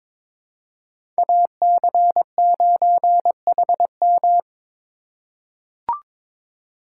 Morse code, heard as a keyed tone, about 22 wpm, 700 Hz. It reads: AC9HM
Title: USA Call Signs - 2x Character Spacing - 22wpm